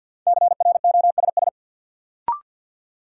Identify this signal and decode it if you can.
Morse code, heard as a keyed tone, about 52 wpm, 700 Hz. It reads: CROSS
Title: Sentences from Top 300 Words - New Word Drill - 50wpm